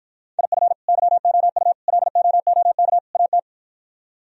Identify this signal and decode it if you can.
Morse code, heard as a keyed tone, about 52 wpm, 700 Hz. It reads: IF YOU LOOK AT